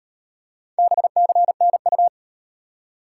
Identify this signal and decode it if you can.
Morse code, heard as a keyed tone, about 33 wpm, 700 Hz. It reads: BCNU